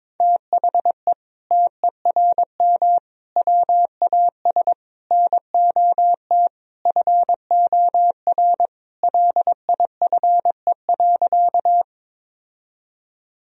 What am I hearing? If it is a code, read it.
Morse code, heard as a keyed tone, about 22 wpm, 700 Hz. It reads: THE TERM WAS NOT FOR LIFE.
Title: Sentences from Top 1000 Words - Review Part 4 - 22wpm